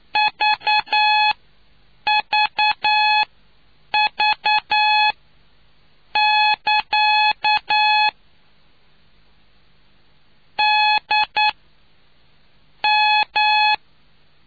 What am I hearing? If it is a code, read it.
Morse code, heard as a keyed tone, about 9 wpm, spaced as for 6 wpm, 850 Hz. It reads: VVV<KA> DM